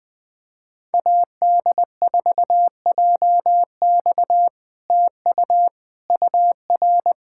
Morse code, heard as a keyed tone, about 19 wpm, 700 Hz. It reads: AD4JX TU UR
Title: POTA QSO Practice - 20wpm